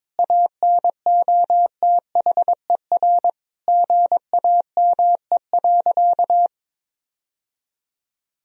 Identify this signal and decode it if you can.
Morse code, heard as a keyed tone, about 22 wpm, 700 Hz. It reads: ANOTHER GAME.
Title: Sentences from Top 1000 Words - Review Part 4 - 22wpm